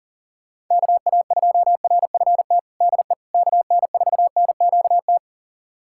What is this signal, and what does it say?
Morse code, heard as a keyed tone, about 37 wpm, 700 Hz. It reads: KA2RFT DE KN4NQT
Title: USA Callsign DE Callsign 40wpm